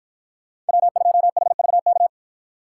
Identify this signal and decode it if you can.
Morse code, heard as a keyed tone, about 53 wpm, 700 Hz. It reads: W2HVK